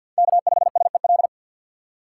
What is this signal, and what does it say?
Morse code, heard as a keyed tone, about 48 wpm, 700 Hz. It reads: K5SEL
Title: USA Call Signs - 5 Characters - 3x Character Spacing - 50wpm